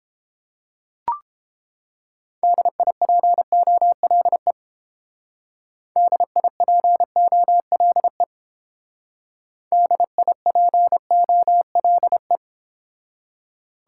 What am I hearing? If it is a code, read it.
Morse code, heard as a keyed tone, about 28 wpm, 700 Hz. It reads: DIPOLE DIPOLE DIPOLE